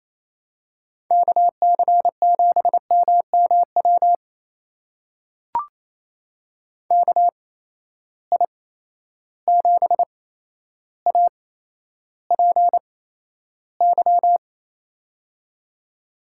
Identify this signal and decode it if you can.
Morse code, heard as a keyed tone, about 27 wpm, spaced as for 12 wpm, 700 Hz. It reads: KC7MMW KI7APY
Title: USA Call Signs - 6 Characters - 8x Character Spacing - 28wpm